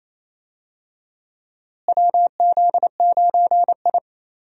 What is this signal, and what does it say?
Morse code, heard as a keyed tone, about 28 wpm, 700 Hz. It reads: WZ9I
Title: USA Call Signs - 4 Characters - Repeated 3x - 28wpm